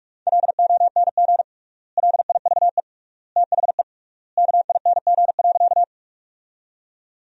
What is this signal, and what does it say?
Morse code, heard as a keyed tone, about 42 wpm, 700 Hz. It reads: LONG LIVE THE KING.
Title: 3x Word-Spacing - Sentences from Top 400 Words - 45wpm